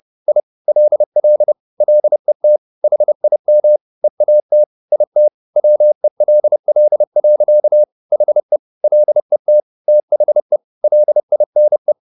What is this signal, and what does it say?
Morse code, heard as a keyed tone, about 30 wpm, 600 Hz. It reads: I LL LET HIM EAT IT WELL. HE LET THE LINE
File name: the old man and the sea / the old man and the sea 1 30wpm